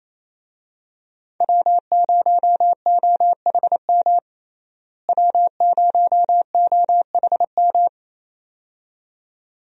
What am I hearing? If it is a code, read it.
Morse code, heard as a keyed tone, about 28 wpm, 700 Hz. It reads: W0OHM W0OHM
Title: USA Call Signs - 5 Characters - Repeated 2x - 28wpm